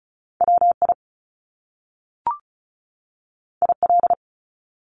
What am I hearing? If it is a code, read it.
Morse code, heard as a keyed tone, about 36 wpm, 700 Hz. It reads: WI IL
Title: US States 35wpm